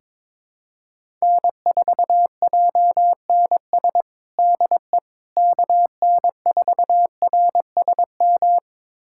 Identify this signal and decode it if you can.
Morse code, heard as a keyed tone, about 22 wpm, 700 Hz. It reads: N4JNS DE KN4RSM
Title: USA Callsign DE Callsign 22wpm